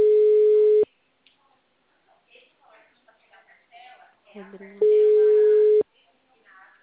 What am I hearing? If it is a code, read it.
Ringback tone (Europe, 425 Hz), with speech.